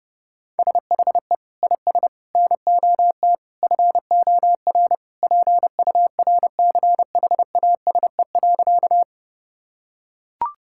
Morse code, heard as a keyed tone, about 30 wpm, 700 Hz. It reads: SHE IS NOT FOR PURCHASE.
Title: Sentences from Top 1000 Words - Review Part 5 - 30wpm